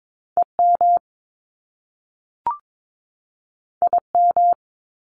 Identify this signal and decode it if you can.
Morse code, heard as a keyed tone, about 22 wpm, 700 Hz. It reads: EM IM
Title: Top Two Letter Combinations in Words 22wpm